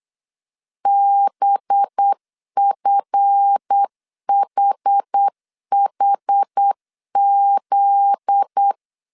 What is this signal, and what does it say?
Morse code, heard as a keyed tone, about 8 wpm, 800 Hz. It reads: BFHHZ
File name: CW / DET / 1 9 1500C S C 8wpm 5653